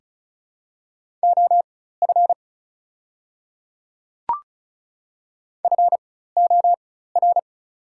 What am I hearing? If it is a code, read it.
Morse code, heard as a keyed tone, about 36 wpm, spaced as for 16 wpm, 690 Hz. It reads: OF FOR
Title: 4x Character Spacing - Top 100 Words - 35wpm